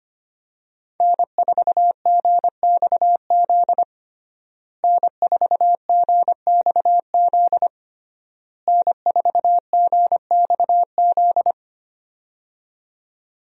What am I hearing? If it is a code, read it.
Morse code, heard as a keyed tone, about 25 wpm, 700 Hz. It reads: N4GXZ N4GXZ N4GXZ
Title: USA Call Signs - 5 Characters - Repeated 3x - 25wpm